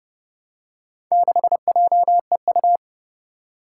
Morse code, heard as a keyed tone, about 30 wpm, 700 Hz. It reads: 6JEU
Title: Four Challenging Letter-Number Combinations 30wpm